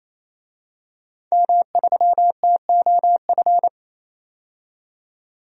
Morse code, heard as a keyed tone, about 28 wpm, 700 Hz. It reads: M3TOF
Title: DXCC-100 Call Signs - 5 Characters - 28wpm